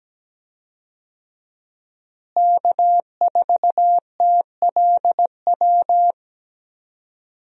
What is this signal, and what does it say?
Morse code, heard as a keyed tone, about 17 wpm, 700 Hz. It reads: K4TLW
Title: USA Call Signs - 5 Characters - Repeated 2x - 17wpm